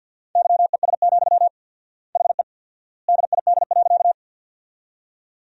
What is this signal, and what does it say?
Morse code, heard as a keyed tone, about 50 wpm, 700 Hz. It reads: YES, HE DID.
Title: 4x Word-Spacing - Sentences from Top 500 Words - 50wpm